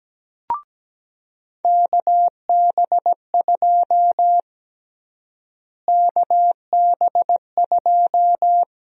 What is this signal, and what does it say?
Morse code, heard as a keyed tone, about 17 wpm, 700 Hz. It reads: KB2 KB2
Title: USA Call Sign Prefixes - Repeated 2x - 17wpm